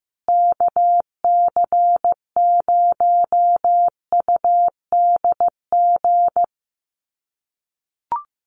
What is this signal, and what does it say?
Morse code, heard as a keyed tone, about 15 wpm, 700 Hz. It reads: KC0UDG
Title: USA Call Signs Repeated 3x 15wpm